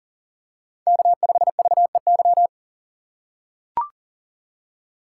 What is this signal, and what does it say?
Morse code, heard as a keyed tone, about 40 wpm, 700 Hz. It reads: K5VEY
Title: USA Call Signs - 5 Characters - 3x Character Spacing - 40wpm